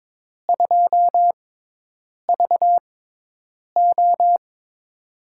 Morse code, heard as a keyed tone, about 22 wpm, spaced as for 8 wpm, 700 Hz. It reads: 2VO
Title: USA Call Signs - 6x Character Spacing - 22wpm